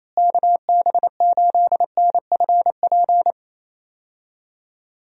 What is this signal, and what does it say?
Morse code, heard as a keyed tone, about 28 wpm, 700 Hz. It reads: KB8NFP
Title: USA Call Signs - 6 Characters - Repeated 2x - 28wpm